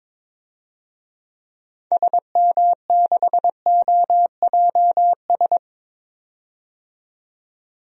Morse code, heard as a keyed tone, about 22 wpm, 700 Hz. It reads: SM6OJS
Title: DXCC-100 Call Signs - 6 Characters - 22wpm